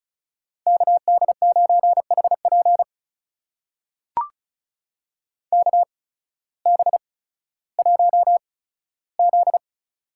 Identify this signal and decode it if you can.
Morse code, heard as a keyed tone, about 36 wpm, spaced as for 16 wpm, 700 Hz. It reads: KD9HP KB1Z
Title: USA Call Signs - 8x Character Spacing - 35wpm